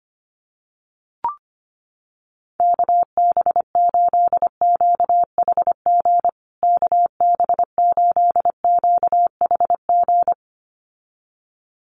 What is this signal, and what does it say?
Morse code, heard as a keyed tone, about 25 wpm, 700 Hz. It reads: KB8QHG KB8QHG
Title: USA Call Signs Repeated 2x 25wpm